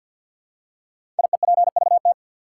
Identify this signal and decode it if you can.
Morse code, heard as a keyed tone, about 53 wpm, 700 Hz. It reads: IEPVT
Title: Code Groups - 5 Letters - 50wpm